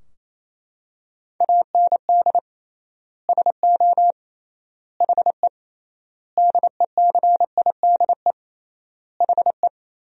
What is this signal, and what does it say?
Morse code, heard as a keyed tone, about 27 wpm, 700 Hz. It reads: AND SO HE DECIDE HE
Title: 3x Word-Spacing - Sentences from Top 500 Words - 28wpm